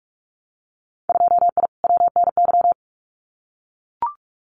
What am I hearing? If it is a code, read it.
Morse code, heard as a keyed tone, about 44 wpm, 700 Hz. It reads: 2I WNY
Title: Winter Field Day Exchanges 45wpm